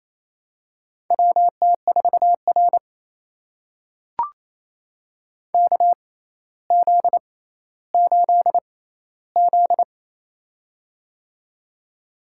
Morse code, heard as a keyed tone, about 28 wpm, spaced as for 14 wpm, 700 Hz. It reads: WT4R KZ8Z
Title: USA Call Signs - 4 Characters - 6x Character Spacing - 28wpm